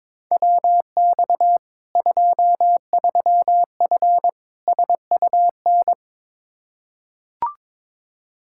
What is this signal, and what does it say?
Morse code, heard as a keyed tone, about 22 wpm, 700 Hz. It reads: WX 23F SUN
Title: Practice QSO - Exchange 2 - Info - 22wpm